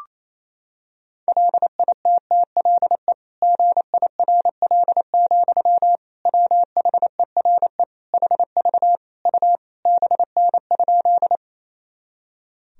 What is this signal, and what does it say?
Morse code, heard as a keyed tone, about 27 wpm, 700 Hz. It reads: LITTLE GIRL, WHERE HV U BN?
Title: Sentences from Top 400 Words and Abbreviations 28wpm